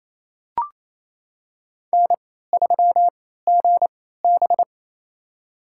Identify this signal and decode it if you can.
Morse code, heard as a keyed tone, about 29 wpm, spaced as for 16 wpm, 700 Hz. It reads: N3GB
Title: USA Call Signs - 4 Characters - 3x Character Spacing - 28wpm